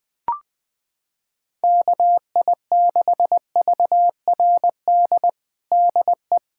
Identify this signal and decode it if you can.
Morse code, heard as a keyed tone, about 21 wpm, 700 Hz. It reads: KI6VRD DE
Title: USA Callsign DE Callsign 20wpm